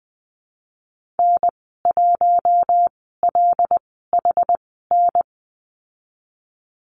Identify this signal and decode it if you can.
Morse code, heard as a keyed tone, about 19 wpm, spaced as for 14 wpm, 700 Hz. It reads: N1LHN
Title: 179 Lesson 29g - Slash - Call Signs 20wpm